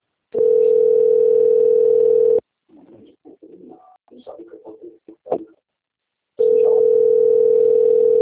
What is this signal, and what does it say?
Ringback tone (North America), 440 + 480 Hz.